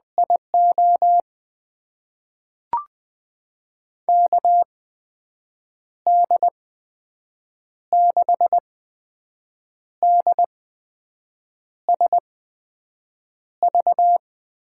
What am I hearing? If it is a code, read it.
Morse code, heard as a keyed tone, about 20 wpm, spaced as for 6 wpm, 700 Hz. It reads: IO KD6DSV
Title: USA Call Signs - 6 Characters - 8x Character Spacing - 20wpm